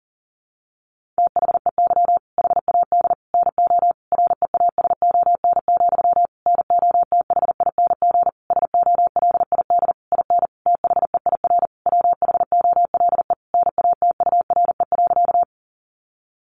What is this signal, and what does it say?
Morse code, heard as a keyed tone, about 40 wpm, 700 Hz. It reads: THEY HAD NO REASON, NOTHING SOLID IN THEIR WHOLE NATURE.
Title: Sentences from Top 1500 Words 40wpm